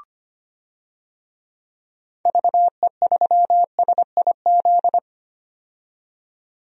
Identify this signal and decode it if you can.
Morse code, heard as a keyed tone, about 25 wpm, 700 Hz. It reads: VE3SIZ
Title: DXCC-100 Call Signs - 6 Characters - 25wpm